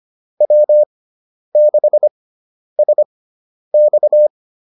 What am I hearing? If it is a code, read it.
Morse code, heard as a keyed tone, about 25 wpm, spaced as for 10 wpm, 600 Hz. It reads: W6SX
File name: QSO102 10